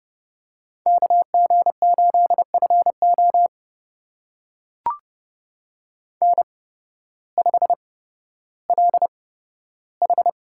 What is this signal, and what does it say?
Morse code, heard as a keyed tone, about 30 wpm, spaced as for 14 wpm, 700 Hz. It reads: KG8FO N5LH
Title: USA Call Signs - 5 Characters - 8x Character Spacing - 30wpm